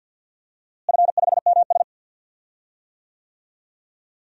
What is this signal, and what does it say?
Morse code, heard as a keyed tone, about 52 wpm, 700 Hz. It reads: F5MS
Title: DXCC-100 Call Signs - 4 Characters - Repeated 3x - 50wpm